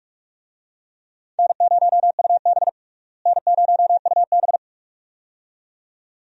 Morse code, heard as a keyed tone, about 44 wpm, 700 Hz. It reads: N0UB N0UB
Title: USA Call Signs - 4 Characters - Repeated 2x - 45wpm